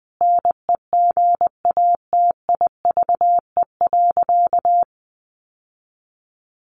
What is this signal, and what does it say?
Morse code, heard as a keyed tone, about 20 wpm, 700 Hz. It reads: NEGATIVE.
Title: Sentences from Top 1000 Words 20wpm